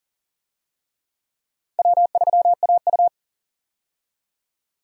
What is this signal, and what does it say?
Morse code, heard as a keyed tone, about 37 wpm, 700 Hz. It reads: W3AU
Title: USA Call Signs - 4 Characters - 40wpm